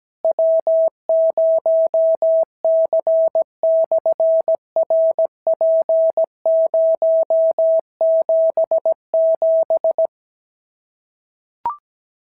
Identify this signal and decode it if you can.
Morse code, heard as a keyed tone, about 17 wpm, 650 Hz. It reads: W0C/RP077
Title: SOTA QSO Practice 17wpm